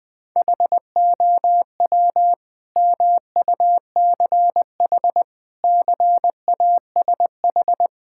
Morse code, heard as a keyed tone, about 21 wpm, 700 Hz. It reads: HOW MUCH CASH
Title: Sentences from Top 1000 Words - Review Part 5 - 20wpm